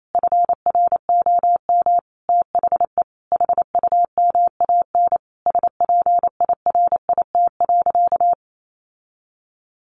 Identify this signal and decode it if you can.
Morse code, heard as a keyed tone, about 28 wpm, 700 Hz. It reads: FROM THE HUMAN SPIRIT.
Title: Sentences from Top 1400 Words 28wpm